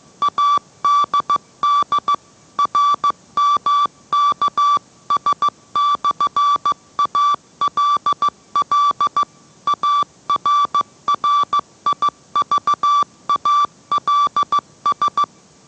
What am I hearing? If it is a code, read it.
Morse code, heard as a keyed tone, about 15 wpm, 1160 Hz. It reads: ADD RMKS/ALL ARRIVALS